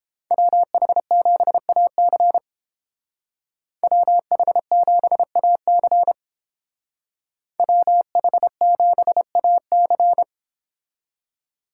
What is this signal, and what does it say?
Morse code, heard as a keyed tone, about 30 wpm, 700 Hz. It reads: WH7AC WH7AC WH7AC